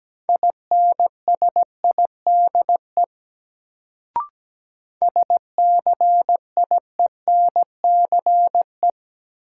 Morse code, heard as a keyed tone, about 17 wpm, 700 Hz. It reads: INSIDE SCIENCE
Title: Sentences from Top 1000 Words - Review Part 2 - New Word Drill - 17wpm